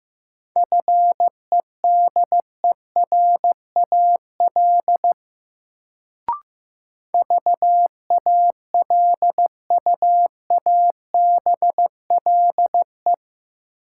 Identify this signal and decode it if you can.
Morse code, heard as a keyed tone, about 15 wpm, 700 Hz. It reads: FEDERAL VALUABLE
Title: Sentences from Top 1100 Words - New Word Drill Alt - 15wpm